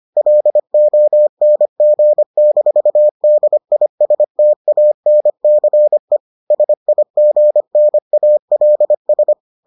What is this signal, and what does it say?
Morse code, heard as a keyed tone, about 25 wpm, 600 Hz. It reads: LONG-DISTANCE SIGNALS